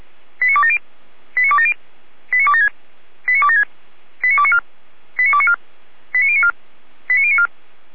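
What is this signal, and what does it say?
Five-tone selective call (ZVEI), heard as quick runs of five tones, tones of about 70 ms. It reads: 89270 89270 89276 89276 89294 89294 89094 89094
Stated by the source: ZVEI five-tone call 89262 received by SDR, logged by Feuerwehr Orlamünde.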